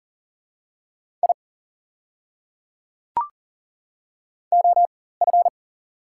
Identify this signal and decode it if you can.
Morse code, heard as a keyed tone, about 40 wpm, spaced as for 19 wpm, 690 Hz. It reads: I OF
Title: 4x Character Spacing - Top 300 Words - 40wpm